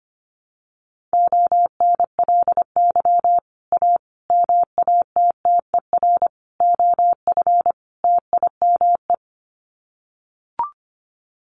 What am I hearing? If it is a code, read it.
Morse code, heard as a keyed tone, about 25 wpm, 700 Hz. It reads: ONLY A MATTER OF TIME
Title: Sets of 5 Words Encore 25wpm